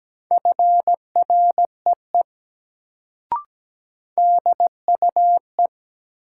Morse code, heard as a keyed tone, about 17 wpm, 700 Hz. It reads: FREE DUE
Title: Sets of 5 Words Encore - New Word Drill - 17wpm